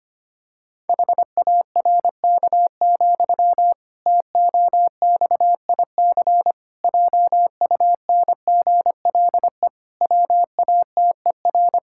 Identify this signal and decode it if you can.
Morse code, heard as a keyed tone, about 25 wpm, 700 Hz. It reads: HARK, TOXIC JUNGLE WATER
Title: Pangram Challenge v2 - 25wpm